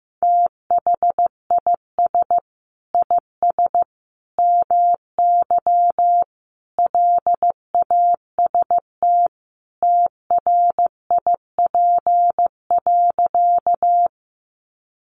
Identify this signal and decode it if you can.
Morse code, heard as a keyed tone, about 15 wpm, 700 Hz. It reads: THIS IS MY LAST TRIP.
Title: Sentences from Top 1300 Words 15wpm